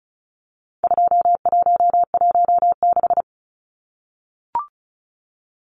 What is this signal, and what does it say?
Morse code, heard as a keyed tone, about 34 wpm, 700 Hz. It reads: 2116
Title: Numbers - 4-Digit 35wpm